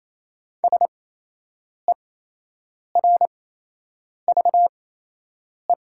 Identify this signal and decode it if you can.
Morse code, heard as a keyed tone, about 28 wpm, spaced as for 8 wpm, 700 Hz. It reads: SERVE
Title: 8x Character Spacing - Top 500 Words - 28wpm